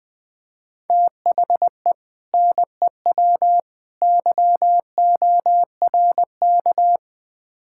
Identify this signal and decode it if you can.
Morse code, heard as a keyed tone, about 21 wpm, 700 Hz. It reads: THE NEW YORK
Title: Build A Phrase - 3-Word Phrases - 20wpm